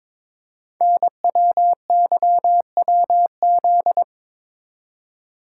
Morse code, heard as a keyed tone, about 23 wpm, 700 Hz. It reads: NWYWZ
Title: Code Groups - 5 Letters - 22wpm